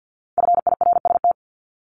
Morse code, heard as a keyed tone, about 50 wpm, 700 Hz. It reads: FIRST
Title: Multi-letter Phonemes 50wpm